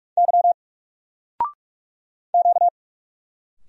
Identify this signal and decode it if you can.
Morse code, heard as a keyed tone, about 44 wpm, 700 Hz. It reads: Y Q
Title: Single Letter-Number 45wpm